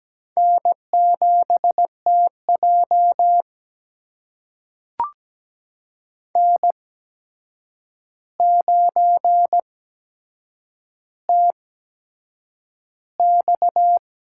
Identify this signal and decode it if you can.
Morse code, heard as a keyed tone, about 17 wpm, spaced as for 7 wpm, 700 Hz. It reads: N7TJ N9TX
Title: USA Call Signs - 4 Characters - 8x Character Spacing - 17wpm